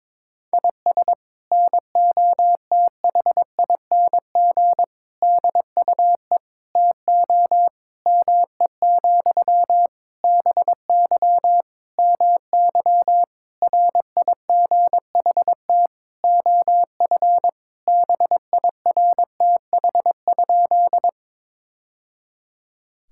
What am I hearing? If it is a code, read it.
Morse code, heard as a keyed tone, about 22 wpm, 700 Hz. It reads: IS NOTHING DUE TO ME, BY MY RIGHT OF BIRTH?
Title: Sentences from Top 1000 Words - Review Part 6 - 22wpm